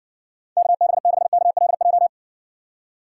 Morse code, heard as a keyed tone, about 60 wpm, 700 Hz. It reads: KB6KLJ